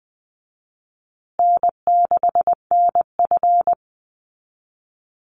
Morse code, heard as a keyed tone, about 20 wpm, 700 Hz. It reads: N6NF
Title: Easy Call Signs Repeated 2x - 20wpm